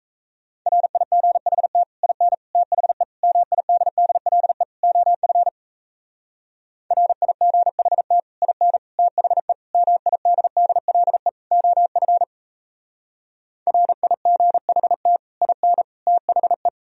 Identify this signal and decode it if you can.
Morse code, heard as a keyed tone, about 37 wpm, 700 Hz. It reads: RIGHT IN THE MIDDLE OF RIGHT IN THE MIDDLE OF RIGHT IN THE